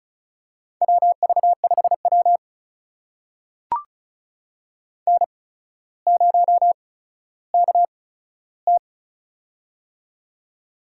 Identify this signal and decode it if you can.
Morse code, heard as a keyed tone, about 36 wpm, spaced as for 15 wpm, 700 Hz. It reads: WV5W N0KT
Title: USA Call Signs - 4 Characters - 8x Character Spacing - 35wpm